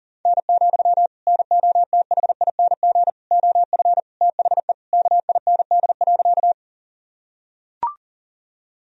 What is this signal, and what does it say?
Morse code, heard as a keyed tone, about 40 wpm, 700 Hz. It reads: N, NOTHING OF THE KIND.
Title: Sentences from Top 500 Words and Abbreviations 40wpm